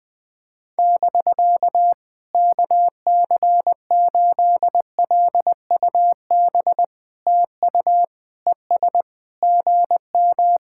Morse code, heard as a keyed tone, about 20 wpm, 700 Hz. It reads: <BK> KC8LUB TU ES GM
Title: POTA QSO Practice - 20wpm